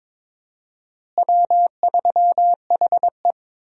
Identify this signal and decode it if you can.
Morse code, heard as a keyed tone, about 21 wpm, 700 Hz. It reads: W3HE
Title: USA Call Signs - 4 Characters - Repeated 3x - 22wpm